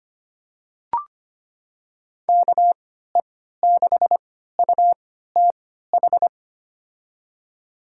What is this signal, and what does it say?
Morse code, heard as a keyed tone, about 25 wpm, spaced as for 14 wpm, 700 Hz. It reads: KE6UTH
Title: USA Call Signs - 6 Characters - 3x Character Spacing - 25wpm